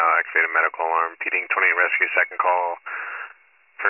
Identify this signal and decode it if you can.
Emergency Alert System (SAME) tones, heard as harsh buzzing bursts, with speech. It is heard from 0.2 to 3.3 s.